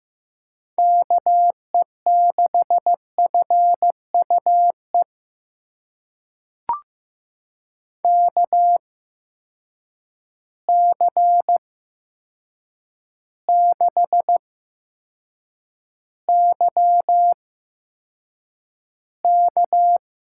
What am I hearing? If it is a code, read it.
Morse code, heard as a keyed tone, about 15 wpm, spaced as for 7 wpm, 700 Hz. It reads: KE6FUE KC6YK
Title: USA Call Signs - 6 Characters - 8x Character Spacing - 15wpm